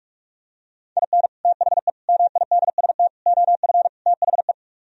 Morse code, heard as a keyed tone, about 44 wpm, 700 Hz. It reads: IN THE MIDST OF THE